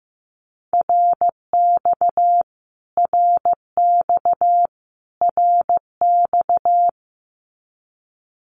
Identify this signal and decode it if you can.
Morse code, heard as a keyed tone, about 15 wpm, 700 Hz. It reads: RX RX RX
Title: Abbreviations Repeated 3x - 15wpm